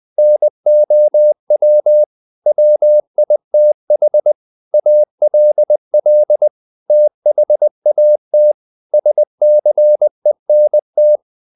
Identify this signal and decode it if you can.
Morse code, heard as a keyed tone, about 20 wpm, 600 Hz. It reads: NOW WITH ALL THAT SCENT
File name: the old man and the sea part 2 20wpm